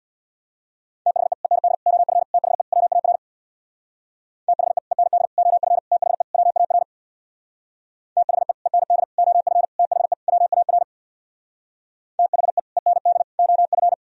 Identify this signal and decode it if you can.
Morse code, heard as a keyed tone, about 62 wpm, 700 Hz. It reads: THE END OF THE WAR THE END OF THE WAR THE END OF THE WAR THE END OF
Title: Speed-Racing - Sets of 5 Words 50wpm